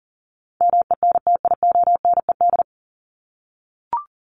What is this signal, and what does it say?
Morse code, heard as a keyed tone, about 40 wpm, 700 Hz. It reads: MENTIONED
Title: ED Suffix - Top 100 Words with Base Words 40wpm